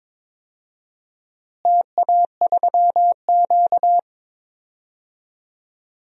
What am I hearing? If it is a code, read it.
Morse code, heard as a keyed tone, about 22 wpm, 700 Hz. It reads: TA3Q
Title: DXCC-100 Call Signs - 4 Characters - Repeated 3x - 22wpm